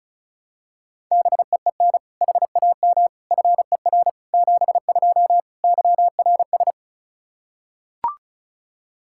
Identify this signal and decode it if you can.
Morse code, heard as a keyed tone, about 36 wpm, 700 Hz. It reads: BEEN HAM FER 72 YRS
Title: Practice QSO - Exchange 3 - More Info - 35wpm